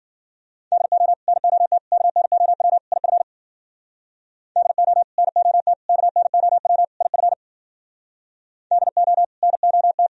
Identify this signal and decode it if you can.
Morse code, heard as a keyed tone, about 52 wpm, 700 Hz. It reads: DO NOT KNOW IF DO NOT KNOW IF DO NOT